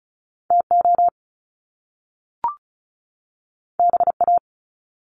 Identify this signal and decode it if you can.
Morse code, heard as a keyed tone, about 36 wpm, 700 Hz. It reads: TO BA
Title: Top Two Letter Combinations in Words 35wpm